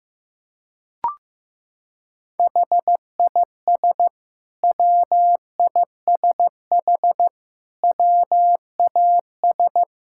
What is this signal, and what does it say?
Morse code, heard as a keyed tone, about 15 wpm, 700 Hz. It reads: HIS WISH WAS
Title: Sentences from Top 1600 Words 15wpm